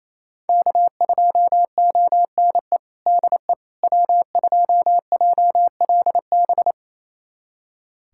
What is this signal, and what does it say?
Morse code, heard as a keyed tone, about 28 wpm, 700 Hz. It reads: K2ONE DE W2JLB
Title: USA Callsign DE Callsign - 5 Characters - 28wpm